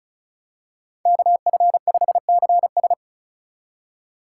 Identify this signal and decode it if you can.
Morse code, heard as a keyed tone, about 36 wpm, 700 Hz. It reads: KF5CS
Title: USA Call Signs - 5 Characters - 3x Character Spacing - 35wpm